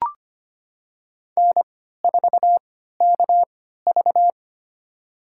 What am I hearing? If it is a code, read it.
Morse code, heard as a keyed tone, about 25 wpm, spaced as for 14 wpm, 700 Hz. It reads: N4KV
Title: USA Call Signs - 4 Characters - 3x Character Spacing - 25wpm